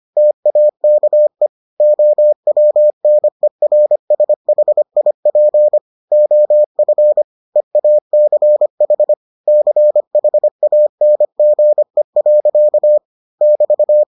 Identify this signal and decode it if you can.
Morse code, heard as a keyed tone, about 25 wpm, 600 Hz. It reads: TAKE OWNERSHIP OF EACH CHANGE. <BT>